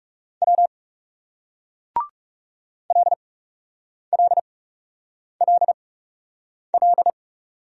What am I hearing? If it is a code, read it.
Morse code, heard as a keyed tone, about 39 wpm, spaced as for 8 wpm, 700 Hz. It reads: W LLLL